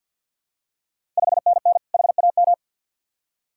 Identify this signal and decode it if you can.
Morse code, heard as a keyed tone, about 50 wpm, 700 Hz. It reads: 5NN HAM
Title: Ontario QSO Party Exchange 50wpm